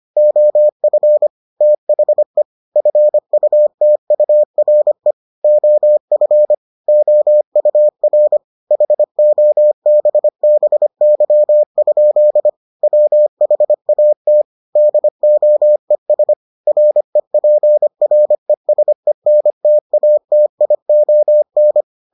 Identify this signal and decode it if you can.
Morse code, heard as a keyed tone, about 25 wpm, 600 Hz. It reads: OF THE FUTURE OF OUR HOBBY? WHAT DOES REPRESENTATION